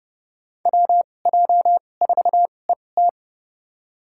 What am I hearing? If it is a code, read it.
Morse code, heard as a keyed tone, about 31 wpm, spaced as for 22 wpm, 700 Hz. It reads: WJ4ET
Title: USA Call Signs - 5 Characters - 2x Character Spacing - 30wpm